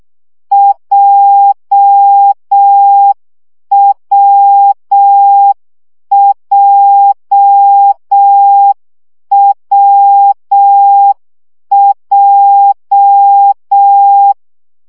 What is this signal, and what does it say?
Morse code, heard as a keyed tone, about 6 wpm, 800 Hz. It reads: JWJWJ
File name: CW gyakorlas w p j 5 baud